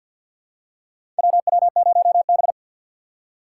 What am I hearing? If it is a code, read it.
Morse code, heard as a keyed tone, about 51 wpm, 700 Hz. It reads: WW0B
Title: USA Call Signs - 4 Characters - 6x Character Spacing - 50wpm